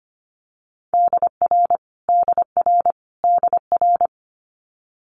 Morse code, heard as a keyed tone, about 25 wpm, 700 Hz. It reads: DR DR DR
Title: Multi-letter Phonemes Repeated 3x 25wpm